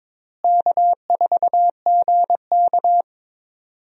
Morse code, heard as a keyed tone, about 22 wpm, 700 Hz. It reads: K4GK
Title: USA Call Signs - 4 Characters - Repeated 3x - 22wpm